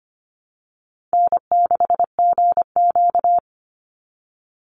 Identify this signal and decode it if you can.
Morse code, heard as a keyed tone, about 25 wpm, 700 Hz. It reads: N6GQ
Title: Easy Call Signs Repeated 2x - 25wpm